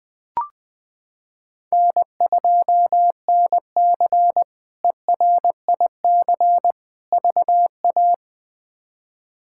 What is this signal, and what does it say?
Morse code, heard as a keyed tone, about 20 wpm, 700 Hz. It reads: N2NC ERIC VA
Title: K1USN SST QSO Practice - 20wpm